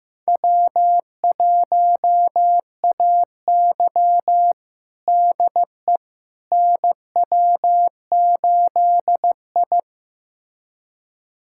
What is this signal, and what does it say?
Morse code, heard as a keyed tone, about 15 wpm, 700 Hz. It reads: W1AY DE NW8I
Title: USA Callsign DE Callsign - 4 Characters - 15wpm